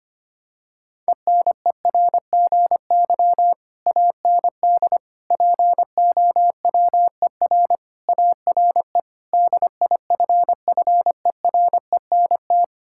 Morse code, heard as a keyed tone, about 25 wpm, 700 Hz. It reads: ENERGY AND POWER ARE DIFFERENT
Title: Sentences from Top 1000 Words - Review Part 6 - 25wpm